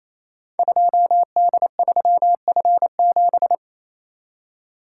Morse code, heard as a keyed tone, about 28 wpm, 700 Hz. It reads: 2D3F7
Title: Code Groups - 5 Letter:Numbers - 28wpm